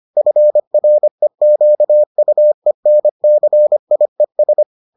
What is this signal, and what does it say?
Morse code, heard as a keyed tone, about 25 wpm, 600 Hz. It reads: FREQUENCIES